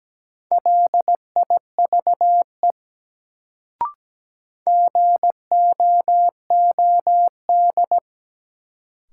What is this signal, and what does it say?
Morse code, heard as a keyed tone, about 17 wpm, 700 Hz. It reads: LIVE GOOD
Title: Sentences from Top 200 Words - New Word Drill - 17wpm